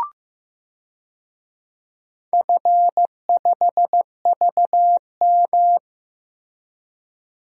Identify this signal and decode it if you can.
Morse code, heard as a keyed tone, about 16 wpm, 700 Hz. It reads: F5VM
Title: DXCC-100 Call Signs - 4 Characters - Repeated 3x - 15wpm